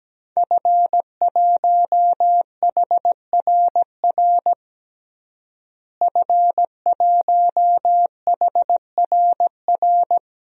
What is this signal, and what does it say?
Morse code, heard as a keyed tone, about 17 wpm, 700 Hz. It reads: F1HRR F1HRR
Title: DXCC-100 Call Signs - 5 Characters - Repeated 3x - 17wpm